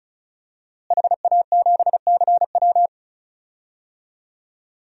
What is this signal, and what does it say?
Morse code, heard as a keyed tone, about 36 wpm, 690 Hz. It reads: HA7CW
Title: DXCC-100 Call Signs - 5 Characters - Repeated 3x - 35wpm